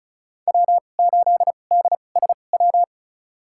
Morse code, heard as a keyed tone, about 34 wpm, spaced as for 25 wpm, 700 Hz. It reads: W8DSW
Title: USA Call Signs - 2x Character Spacing - 35wpm